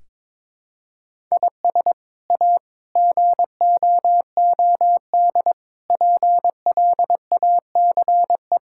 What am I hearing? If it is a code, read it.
Morse code, heard as a keyed tone, about 22 wpm, 700 Hz. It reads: IS A GOOD PLACE
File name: practice / 22wpm / Sets of 6 Words - 22wpm